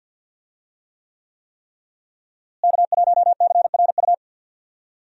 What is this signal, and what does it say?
Morse code, heard as a keyed tone, about 46 wpm, 700 Hz. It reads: K1CRU
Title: USA Call Signs - 5 Characters - Repeated 3x - 50wpm